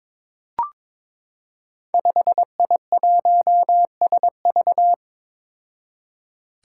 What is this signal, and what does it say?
Morse code, heard as a keyed tone, about 23 wpm, 700 Hz. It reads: 5I1SV
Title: Five Challenging Letter-Number Combinations 22wpm